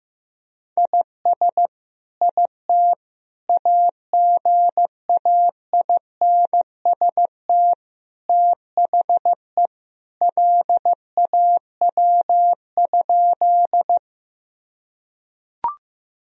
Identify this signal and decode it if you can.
Morse code, heard as a keyed tone, about 15 wpm, 700 Hz. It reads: IS IT AGAINST THE LAW?
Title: Sentences from Top 1000 Words - Review Part 3 - 15wpm